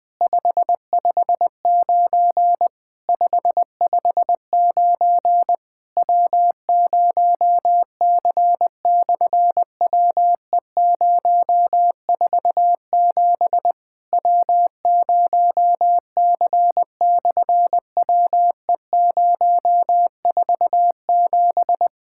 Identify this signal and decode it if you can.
Morse code, heard as a keyed tone, about 20 wpm, 700 Hz. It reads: 559 559 W0C/WE047 W0C/WE047
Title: SOTA QSO Practice 20wpm